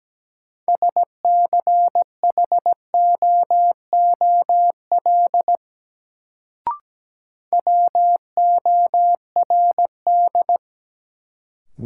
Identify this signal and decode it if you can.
Morse code, heard as a keyed tone, about 17 wpm, 700 Hz. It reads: SCHOOL WORD
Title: Sentences from Top 300 Words - New Word Drill - 17wpm